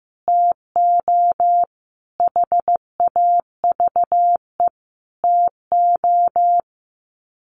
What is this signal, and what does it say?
Morse code, heard as a keyed tone, about 15 wpm, 700 Hz. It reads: TO HAVE TO
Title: Sets of 5 Words 15wpm